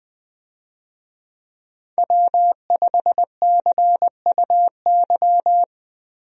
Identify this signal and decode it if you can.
Morse code, heard as a keyed tone, about 20 wpm, 700 Hz. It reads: W5CUY
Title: USA Call Signs - 5 Characters - Repeated 2x - 20wpm